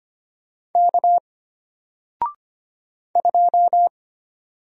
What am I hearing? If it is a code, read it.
Morse code, heard as a keyed tone, about 25 wpm, 700 Hz. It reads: K 2
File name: Single-Letter-Number-25wpm